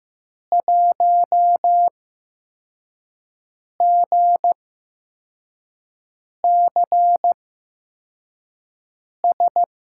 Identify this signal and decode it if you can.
Morse code, heard as a keyed tone, about 15 wpm, spaced as for 4 wpm, 700 Hz. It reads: 1GCS
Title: USA Call Signs - 8x Character Spacing - 15wpm